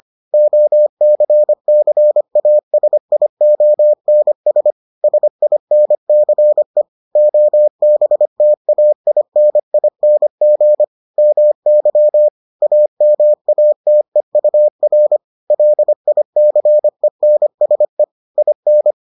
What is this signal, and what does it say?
Morse code, heard as a keyed tone, about 25 wpm, 600 Hz. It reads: OCCASIONS SINCE OBTAINING MY AMATEUR LICENSE IN